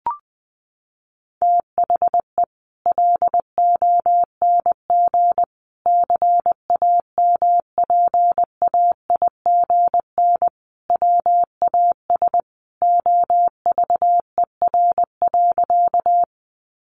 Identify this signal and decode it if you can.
Morse code, heard as a keyed tone, about 19 wpm, 700 Hz. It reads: THE LONG CAMPAIGN WAS OVER.
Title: Sentences from Top 1200 Words 20wpm